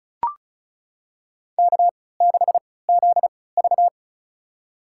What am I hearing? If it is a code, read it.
Morse code, heard as a keyed tone, about 36 wpm, spaced as for 20 wpm, 700 Hz. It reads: K6ZV
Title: USA Call Signs - 4 Characters - 3x Character Spacing - 35wpm